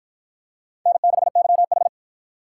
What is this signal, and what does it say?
Morse code, heard as a keyed tone, about 48 wpm, 700 Hz. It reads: N6YH